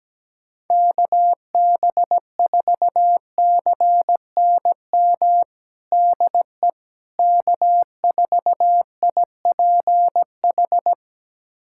Morse code, heard as a keyed tone, about 17 wpm, 700 Hz. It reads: KB4CNM DE K4IPH
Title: USA Callsign DE Callsign 17wpm